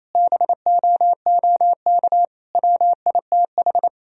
Morse code, heard as a keyed tone, about 28 wpm, 700 Hz. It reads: BOOK WITH
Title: Sentences from Top 1000 Words - Review Part 3 - 28wpm